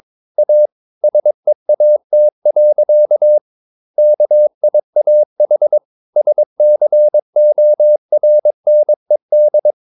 Morse code, heard as a keyed tone, about 22 wpm, 600 Hz. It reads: A SEAT. KIAH SCORNED
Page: Romanen Never af Ken Follet på engelsk.